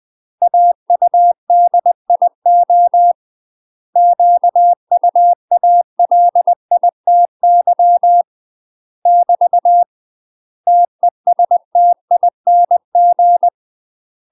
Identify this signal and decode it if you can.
Morse code, heard as a keyed tone, about 20 wpm, 700 Hz. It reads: AUDIO QUALITY <BT> TESTING
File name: UK MASTER QSOs TRAINING / 20WPM / UK MASTER QSOs 0023